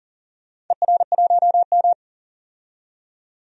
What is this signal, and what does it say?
Morse code, heard as a keyed tone, about 39 wpm, 700 Hz. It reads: ER1M
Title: DXCC-100 Call Signs - 4 Characters - 40wpm